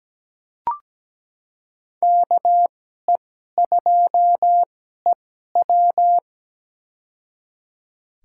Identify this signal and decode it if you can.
Morse code, heard as a keyed tone, about 17 wpm, spaced as for 12 wpm, 700 Hz. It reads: KE2EW
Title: USA Call Signs - 2x Character Spacing - 17wpm